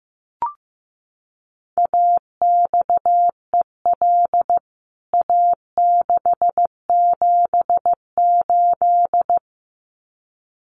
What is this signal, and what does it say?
Morse code, heard as a keyed tone, about 15 wpm, 700 Hz. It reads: AXEL A678
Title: CWT Exchange 15wpm